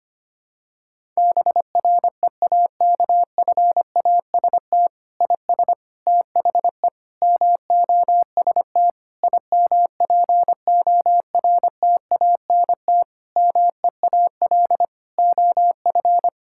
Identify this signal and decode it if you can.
Morse code, heard as a keyed tone, about 25 wpm, 700 Hz. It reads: BREAKFAST IS THE MOST IMPORTANT MEAL OF
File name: Sets of 9 Words - 25wpm